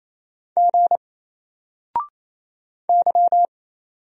Morse code, heard as a keyed tone, about 27 wpm, 700 Hz. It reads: G Y
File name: Single-Letter-Number-28wpm